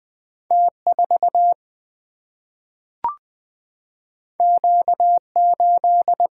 Morse code, heard as a keyed tone, about 20 wpm, 700 Hz. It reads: T4 Q8
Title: Single-Letter Single-Number 20wpm